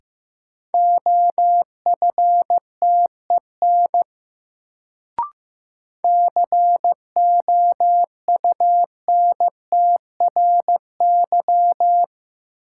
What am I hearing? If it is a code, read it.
Morse code, heard as a keyed tone, about 15 wpm, 700 Hz. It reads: OFTEN COUNTRY
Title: Sentences from Top 300 Words - New Word Drill - 15wpm